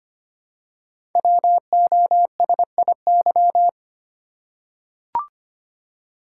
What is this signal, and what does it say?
Morse code, heard as a keyed tone, about 25 wpm, 700 Hz. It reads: WOSIY
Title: Code Groups - 5 Letter:Numbers - 25wpm